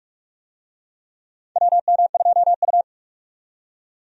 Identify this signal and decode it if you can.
Morse code, heard as a keyed tone, about 44 wpm, 700 Hz. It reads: WM2U
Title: USA Call Signs - 4 Characters - 45wpm